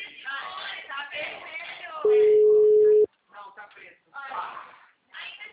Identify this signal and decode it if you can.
Ringback tone (Europe, 425 Hz), with speech.
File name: out- -2001- - - .34160